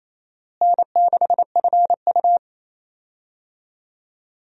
Morse code, heard as a keyed tone, about 28 wpm, 700 Hz. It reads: N6FU
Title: USA Call Signs - 4 Characters - Repeated 3x - 28wpm